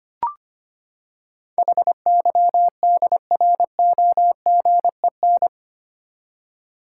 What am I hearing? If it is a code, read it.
Morse code, heard as a keyed tone, about 25 wpm, 700 Hz. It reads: HYDROGEN
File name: Eight Letter Words - 25wpm